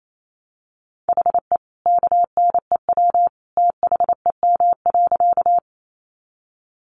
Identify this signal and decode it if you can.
Morse code, heard as a keyed tone, about 28 wpm, 700 Hz. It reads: HE KNEW THEM.
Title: Sentences from Top 400 Words 28wpm